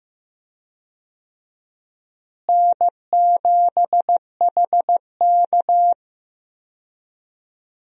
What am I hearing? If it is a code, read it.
Morse code, heard as a keyed tone, about 15 wpm, 700 Hz. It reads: N7HK
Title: USA Call Signs - 4 Characters - Repeated 2x - 15wpm